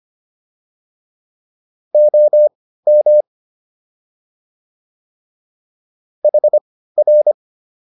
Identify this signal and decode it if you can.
Morse code, heard as a keyed tone, about 25 wpm, spaced as for 15 wpm, 600 Hz. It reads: OM HR